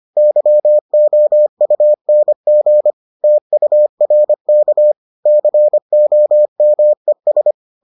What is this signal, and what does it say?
Morse code, heard as a keyed tone, about 25 wpm, 600 Hz. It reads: YOUNG TURK COMES